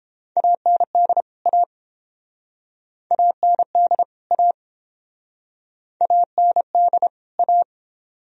Morse code, heard as a keyed tone, about 31 wpm, 700 Hz. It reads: AND A AND A AND A